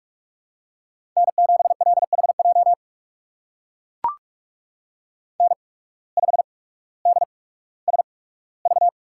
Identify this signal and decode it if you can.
Morse code, heard as a keyed tone, about 44 wpm, spaced as for 19 wpm, 700 Hz. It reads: N7LHJ N5DSV
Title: USA Call Signs - 5 Characters - 8x Character Spacing - 45wpm